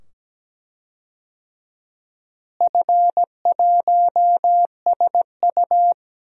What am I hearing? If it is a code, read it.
Morse code, heard as a keyed tone, about 17 wpm, 700 Hz. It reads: F1SU
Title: DXCC-100 Call Signs - 4 Characters - Repeated 2x - 17wpm